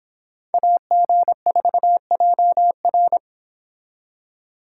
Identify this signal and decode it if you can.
Morse code, heard as a keyed tone, about 31 wpm, 700 Hz. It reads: AG4JR